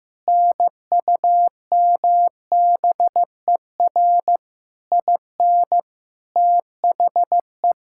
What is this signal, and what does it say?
Morse code, heard as a keyed tone, about 15 wpm, 700 Hz. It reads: NUMBER IN THE
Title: Sentences from Top 1000 Words - Review Part 5 - 15wpm